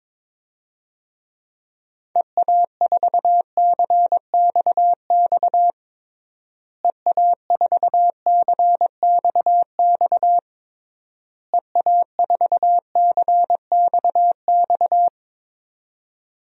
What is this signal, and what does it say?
Morse code, heard as a keyed tone, about 22 wpm, 700 Hz. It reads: EA4CXX EA4CXX EA4CXX
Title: DXCC-100 Call Signs - 6 Characters - Repeated 3x - 22wpm